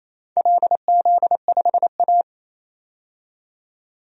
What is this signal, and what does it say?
Morse code, heard as a keyed tone, about 28 wpm, 700 Hz. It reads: LZ5A
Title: DXCC-100 Call Signs - 4 Characters - 28wpm